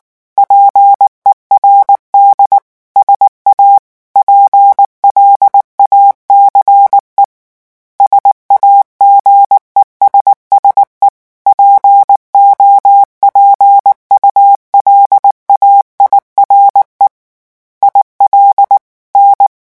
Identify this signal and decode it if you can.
Morse code, heard as a keyed tone, about 19 wpm, 800 Hz. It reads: PERD SA PLACE SAGESSE POPULAIRE IL N
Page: REVISION 8 - 8 mots minute